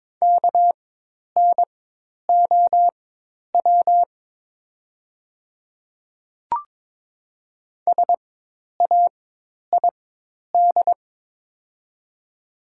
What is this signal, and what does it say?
Morse code, heard as a keyed tone, about 23 wpm, spaced as for 10 wpm, 700 Hz. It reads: KNOW SAID
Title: 4x Character Spacing - Top 100 Words - 22wpm